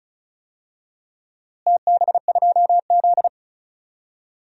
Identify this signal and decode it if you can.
Morse code, heard as a keyed tone, about 36 wpm, 690 Hz. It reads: TB2Z
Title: DXCC-100 Call Signs - 4 Characters - Repeated 2x - 35wpm